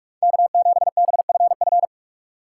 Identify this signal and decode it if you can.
Morse code, heard as a keyed tone, about 46 wpm, 700 Hz. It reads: K7BFF
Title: USA Call Signs - 5 Characters - 2x Character Spacing - 45wpm